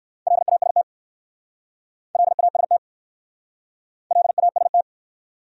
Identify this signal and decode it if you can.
Morse code, heard as a keyed tone, about 59 wpm, 690 Hz. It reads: LAST LAST LAST